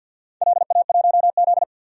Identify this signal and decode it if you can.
Morse code, heard as a keyed tone, about 50 wpm, 700 Hz. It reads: LA1Z
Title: DXCC-100 Call Signs - 4 Characters - 50wpm